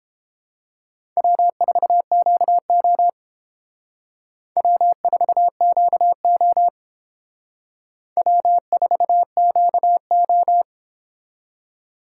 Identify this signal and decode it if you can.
Morse code, heard as a keyed tone, about 28 wpm, 700 Hz. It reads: W4QO W4QO W4QO